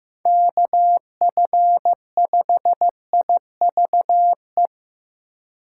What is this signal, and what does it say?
Morse code, heard as a keyed tone, about 15 wpm, 700 Hz. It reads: KF5IVE
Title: USA Call Signs - 2x Character Spacing - 15wpm